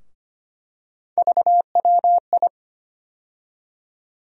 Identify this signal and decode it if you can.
Morse code, heard as a keyed tone, about 25 wpm, 700 Hz. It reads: VWI
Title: USA Call Sign Postfixes - 3 Letters 25wpm